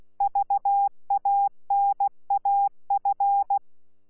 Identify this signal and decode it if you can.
Morse code, heard as a keyed tone, about 16 wpm, 800 Hz. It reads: VANAF
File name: Z PI4VRZ / PI4VRZ 14-jun-2025